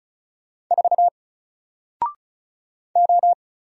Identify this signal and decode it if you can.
Morse code, heard as a keyed tone, about 36 wpm, 700 Hz. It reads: 4 O
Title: Single Letter-Number Alt 35wpm